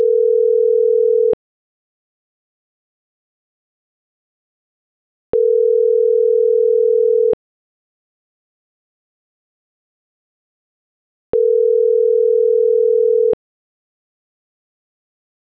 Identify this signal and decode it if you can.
Ringback tone (North America), 440 + 480 Hz.